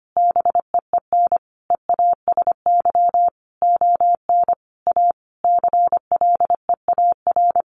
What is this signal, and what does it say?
Morse code, heard as a keyed tone, about 25 wpm, 700 Hz. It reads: BEEN EASY ON A CLEAR
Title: Sentences from Top 800 Words 25wpm